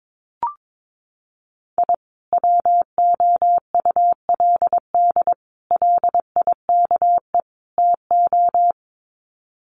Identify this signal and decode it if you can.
Morse code, heard as a keyed tone, about 22 wpm, 700 Hz. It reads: I WOULD LIKE TO
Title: Sets of 4 Words 22wpm